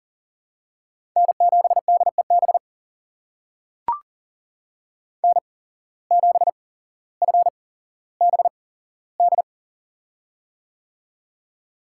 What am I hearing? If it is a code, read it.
Morse code, heard as a keyed tone, about 42 wpm, spaced as for 17 wpm, 700 Hz. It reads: N7DEB N7FBD
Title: USA Call Signs - 5 Characters - 8x Character Spacing - 40wpm